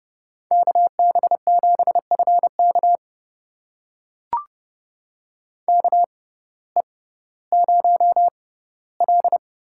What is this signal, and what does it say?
Morse code, heard as a keyed tone, about 30 wpm, spaced as for 17 wpm, 700 Hz. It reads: KB7FK KE0L
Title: USA Call Signs - 5 Characters - 6x Character Spacing - 30wpm